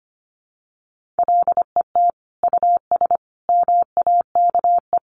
Morse code, heard as a keyed tone, about 25 wpm, 700 Hz. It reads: LET US MAKE
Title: Sentences from Top 1500 Words 25wpm